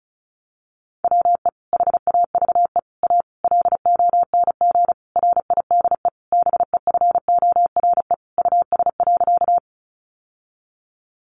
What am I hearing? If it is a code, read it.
Morse code, heard as a keyed tone, about 34 wpm, 700 Hz. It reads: WE HAVE A LONG RIDE BEFORE US.
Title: Sentences from Top 1200 Words 35wpm